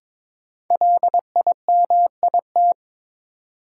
Morse code, heard as a keyed tone, about 22 wpm, 700 Hz. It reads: LIMIT
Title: Sentences from Top 1500 Words - New Word Drill Alt - 22wpm